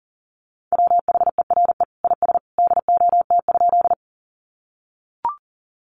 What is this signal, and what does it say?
Morse code, heard as a keyed tone, about 38 wpm, 700 Hz. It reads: WHERE IS DOT?
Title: Sentences from Top 1500 Words 40wpm